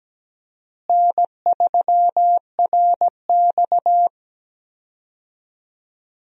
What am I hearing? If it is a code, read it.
Morse code, heard as a keyed tone, about 17 wpm, 700 Hz. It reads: N3RX
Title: USA Call Signs - 4 Characters - Repeated 3x - 17wpm